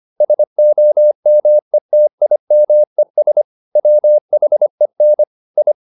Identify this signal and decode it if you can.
Morse code, heard as a keyed tone, about 25 wpm, 600 Hz. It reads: SOMETIMES WHEN I